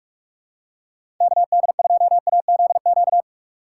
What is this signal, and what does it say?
Morse code, heard as a keyed tone, about 47 wpm, 700 Hz. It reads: KD2AZQ
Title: USA Call Signs - 6 Characters - 6x Character Spacing - 45wpm